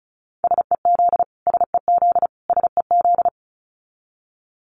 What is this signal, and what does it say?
Morse code, heard as a keyed tone, about 34 wpm, 700 Hz. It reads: SEZ SEZ SEZ
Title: Abbreviations Repeated 3x - 35wpm